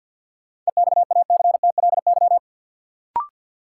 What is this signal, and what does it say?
Morse code, heard as a keyed tone, about 51 wpm, 700 Hz. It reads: EXACTLY
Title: Sentences from Top 900 Words - New Word Drill - 50wpm